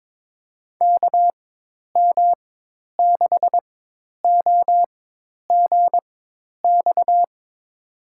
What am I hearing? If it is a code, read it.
Morse code, heard as a keyed tone, about 22 wpm, spaced as for 10 wpm, 700 Hz. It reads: KM6OGX
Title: USA Call Signs - 6 Characters - 4x Character Spacing - 22wpm